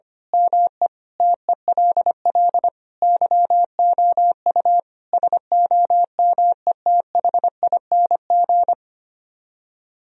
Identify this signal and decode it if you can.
Morse code, heard as a keyed tone, about 24 wpm, 700 Hz. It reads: ME TELL YOU SOMETHING
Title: Speed-Racing - Sets of 5 Words 25wpm